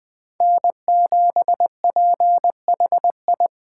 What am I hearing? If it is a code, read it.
Morse code, heard as a keyed tone, about 20 wpm, 700 Hz. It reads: N7PHI
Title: K1USN SST QSO Practice - 20wpm